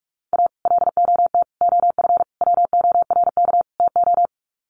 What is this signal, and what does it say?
Morse code, heard as a keyed tone, about 44 wpm, 700 Hz. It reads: A LOT OF WORK TO
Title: Sets of 5 Words Encore2 45wpm